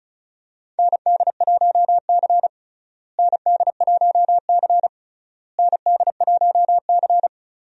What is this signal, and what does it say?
Morse code, heard as a keyed tone, about 34 wpm, 700 Hz. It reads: ND1C ND1C ND1C
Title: USA Call Signs - 4 Characters - Repeated 3x - 35wpm